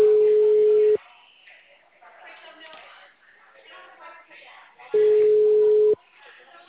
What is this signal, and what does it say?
Ringback tone (Europe, 425 Hz), with speech.